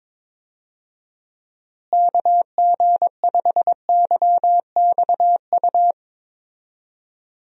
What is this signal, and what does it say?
Morse code, heard as a keyed tone, about 22 wpm, 700 Hz. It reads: KG5YXU
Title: USA Call Signs - 6 Characters - Repeated 3x - 22wpm